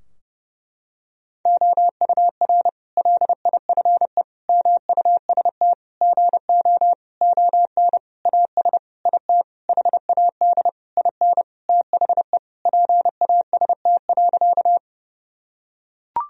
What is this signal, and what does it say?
Morse code, heard as a keyed tone, about 32 wpm, 700 Hz. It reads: OUR LIFE MUST GO ON AS IT HAD IN THE PAST.
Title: Sentences from Top 1000 Words - Review Part 3 - 30wpm